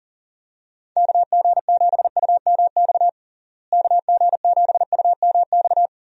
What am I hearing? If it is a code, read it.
Morse code, heard as a keyed tone, about 40 wpm, 700 Hz. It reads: KG7UMX KG7UMX
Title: USA Call Signs - 6 Characters - Repeated 3x - 40wpm